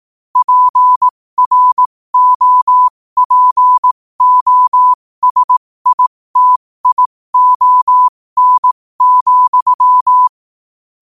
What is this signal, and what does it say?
Morse code, heard as a keyed tone, about 18 wpm, spaced as for 15 wpm, 1000 Hz. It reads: PROPOSITION,
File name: qotdfiles / 15wpm